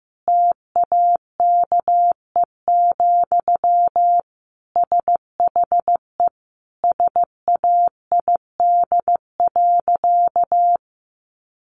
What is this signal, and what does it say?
Morse code, heard as a keyed tone, about 15 wpm, 700 Hz. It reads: TAKE, SHE SAID.
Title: Sentences from Top 1500 Words 15wpm